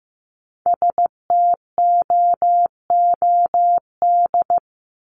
Morse code, heard as a keyed tone, about 15 wpm, 700 Hz. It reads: STOOD
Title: Top 500 Words 15wpm